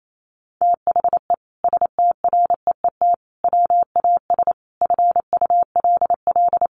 Morse code, heard as a keyed tone, about 28 wpm, 700 Hz. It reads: THE STREET WAS FULL
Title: Sentences from Top 1300 Words 28wpm